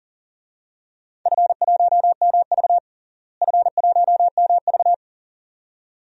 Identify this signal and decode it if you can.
Morse code, heard as a keyed tone, about 40 wpm, 700 Hz. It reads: F1MV F1MV
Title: DXCC-100 Call Signs - 4 Characters - Repeated 2x - 40wpm